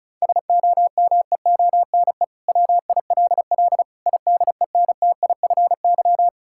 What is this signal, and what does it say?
Morse code, heard as a keyed tone, about 34 wpm, 690 Hz. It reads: SOMEONE WILL IDENTIFY
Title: Sentences from Top 1000 Words - Review Part 6 - 35wpm